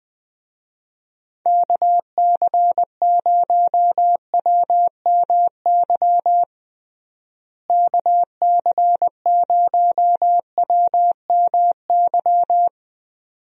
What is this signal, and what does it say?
Morse code, heard as a keyed tone, about 20 wpm, 700 Hz. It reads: KC0WMY KC0WMY
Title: USA Call Signs - 6 Characters - Repeated 2x - 20wpm